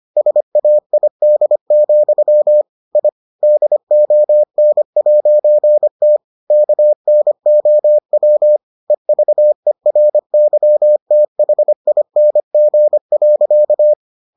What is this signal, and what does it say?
Morse code, heard as a keyed tone, about 25 wpm, 600 Hz. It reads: SAID, I DON'T KNOW EVERYTHING.